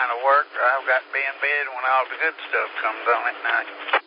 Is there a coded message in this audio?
Yes: Emergency Alert System (SAME) tones from 0.8 to 3.7 s, heard as harsh buzzing bursts, with speech.